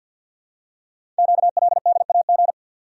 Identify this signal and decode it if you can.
Morse code, heard as a keyed tone, about 49 wpm, 700 Hz. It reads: QLDAG
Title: Code Groups - 5 Letters - 50wpm